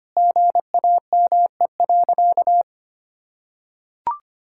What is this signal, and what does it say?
Morse code, heard as a keyed tone, about 25 wpm, 700 Hz. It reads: GAME.
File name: practice / 25wpm / Sentences with 3-Word Phrases - 25wpm